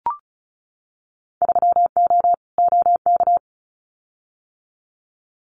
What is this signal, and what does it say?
Morse code, heard as a keyed tone, about 34 wpm, 700 Hz. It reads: 3O OK
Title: Winter Field Day Exchanges 35wpm